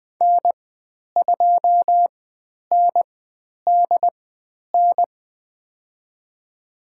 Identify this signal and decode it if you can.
Morse code, heard as a keyed tone, about 20 wpm, spaced as for 10 wpm, 700 Hz. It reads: N2NDN
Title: 109 Lesson 18g - Letter m - Call Signs 20wpm